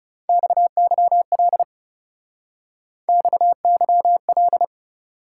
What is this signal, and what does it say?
Morse code, heard as a keyed tone, about 36 wpm, 700 Hz. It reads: XYL XYL
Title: Speed-Racing Top 90 QSO Elements 30wpm